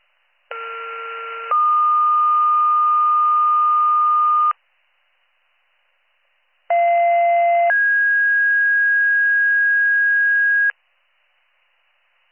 Two-tone page, heard as a one-second tone, then a three-second tone, about 1370 then 1220 Hz, about 690 then 1650 Hz. It is heard from 0.5 to 10.7 s.